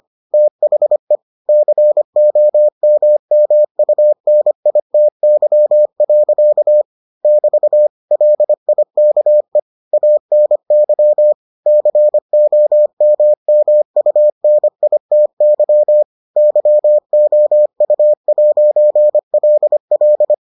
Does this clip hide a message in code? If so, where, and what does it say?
Yes: Morse code from 0.3 to 20.3 s, heard as a keyed tone, about 25 wpm, 600 Hz. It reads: THE COMMUNITY. <BT> LIKE ANY COMMUNITY YOU'LL